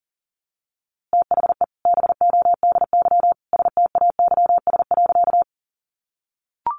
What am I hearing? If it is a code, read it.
Morse code, heard as a keyed tone, about 40 wpm, 700 Hz. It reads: THE BODY STAYS.
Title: A Sound Of Thunder 40wpm